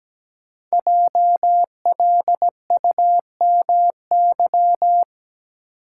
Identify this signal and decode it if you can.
Morse code, heard as a keyed tone, about 17 wpm, 700 Hz. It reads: JLUMY
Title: Code Groups - 5 Letters - 17wpm